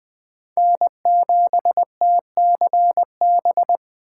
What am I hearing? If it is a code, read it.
Morse code, heard as a keyed tone, about 21 wpm, 700 Hz. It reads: N7TCB
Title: USA Callsign DE Callsign - 5 Characters - 20wpm